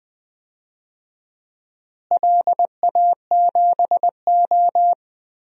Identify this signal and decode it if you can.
Morse code, heard as a keyed tone, about 20 wpm, 700 Hz. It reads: LA7O
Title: DXCC-100 Call Signs - 4 Characters - Repeated 3x - 20wpm